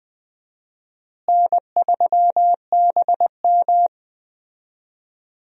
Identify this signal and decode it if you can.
Morse code, heard as a keyed tone, about 20 wpm, 700 Hz. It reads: N3BM
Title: USA Call Signs - 4 Characters - Repeated 2x - 20wpm